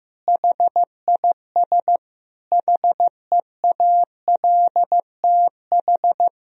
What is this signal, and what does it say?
Morse code, heard as a keyed tone, about 15 wpm, 700 Hz. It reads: HIS HEALTH
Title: Sentences from Top 1000 Words - Review Part 3 - 15wpm